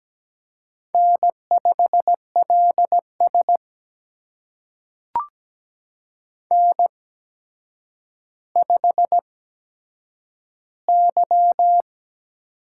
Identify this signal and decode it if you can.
Morse code, heard as a keyed tone, about 17 wpm, spaced as for 8 wpm, 700 Hz. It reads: N5LS N5Y
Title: USA Call Signs - 4 Characters - 8x Character Spacing - 17wpm